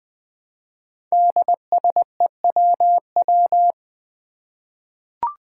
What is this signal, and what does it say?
Morse code, heard as a keyed tone, about 20 wpm, 700 Hz. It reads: DSEWW
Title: Five Challenging Letter-Number Combinations 20wpm